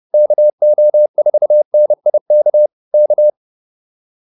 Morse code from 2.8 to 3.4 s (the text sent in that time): K